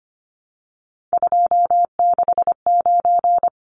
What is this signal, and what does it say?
Morse code, heard as a keyed tone, about 25 wpm, 700 Hz. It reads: 269
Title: Numbers - 3-Digit 25wpm